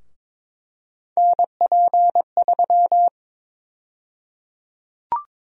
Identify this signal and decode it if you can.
Morse code, heard as a keyed tone, about 22 wpm, 700 Hz. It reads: NP3
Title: USA Call Sign Prefixes - 22wpm